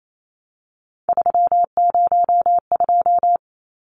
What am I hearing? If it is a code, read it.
Morse code, heard as a keyed tone, about 28 wpm, 700 Hz. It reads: 302
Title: Numbers - 3-Digit 28wpm